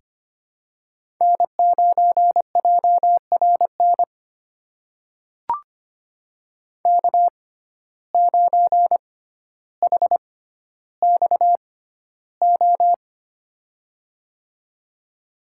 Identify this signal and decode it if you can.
Morse code, heard as a keyed tone, about 25 wpm, spaced as for 13 wpm, 700 Hz. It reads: N9JRN K9HXO
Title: USA Call Signs - 5 Characters - 6x Character Spacing - 25wpm